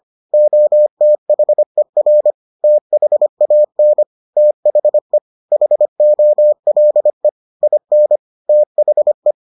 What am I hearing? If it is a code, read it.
Morse code, heard as a keyed tone, about 25 wpm, 600 Hz. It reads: OTHER THAN THE HOLE IN THE